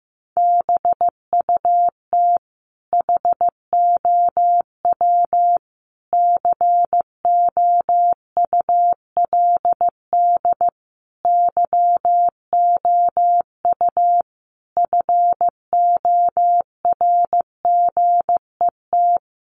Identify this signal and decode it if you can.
Morse code, heard as a keyed tone, about 15 wpm, 700 Hz. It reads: BUT HOW COULD YOU FORGET
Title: Sentences from Top 1200 Words 15wpm